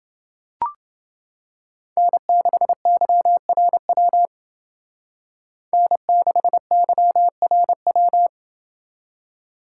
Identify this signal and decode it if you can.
Morse code, heard as a keyed tone, about 29 wpm, 700 Hz. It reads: N6YRW N6YRW